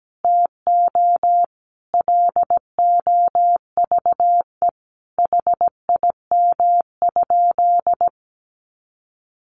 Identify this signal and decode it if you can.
Morse code, heard as a keyed tone, about 17 wpm, 700 Hz. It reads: TO LOVE HIM?
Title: Sentences from Top 1100 Words 17wpm